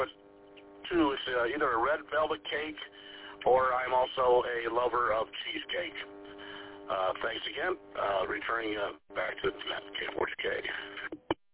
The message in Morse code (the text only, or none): none